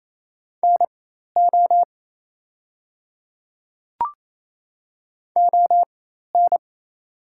Morse code, heard as a keyed tone, about 27 wpm, spaced as for 13 wpm, 690 Hz. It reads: NO ON
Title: 4x Character Spacing - Top 100 Words - 28wpm